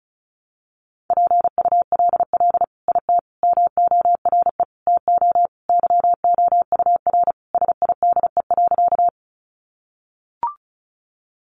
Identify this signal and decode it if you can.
Morse code, heard as a keyed tone, about 34 wpm, 700 Hz. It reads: PULL IT MORE TO YOUR SIDE.
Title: Sentences from Top 400 Words 35wpm